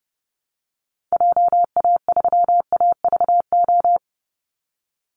Morse code, heard as a keyed tone, about 29 wpm, 700 Hz. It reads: JA3AVO
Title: Call Signs Repeated 3x - 30wpm